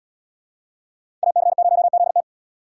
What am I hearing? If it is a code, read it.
Morse code, heard as a keyed tone, about 78 wpm, 700 Hz. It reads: AC0ZT